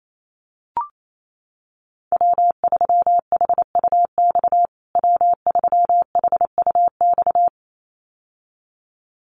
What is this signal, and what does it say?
Morse code, heard as a keyed tone, about 28 wpm, 700 Hz. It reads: W3HUX W3HUX
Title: USA Call Signs Repeated 2x 28wpm